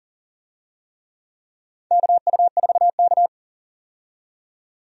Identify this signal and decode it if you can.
Morse code, heard as a keyed tone, about 40 wpm, 700 Hz. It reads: KU4K
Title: USA Call Signs - 4 Characters - 40wpm